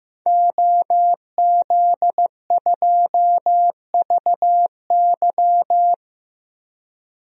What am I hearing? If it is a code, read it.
Morse code, heard as a keyed tone, about 15 wpm, 700 Hz. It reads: OZ2VY
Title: DXCC-100 Call Signs - 5 Characters - Repeated 3x - 15wpm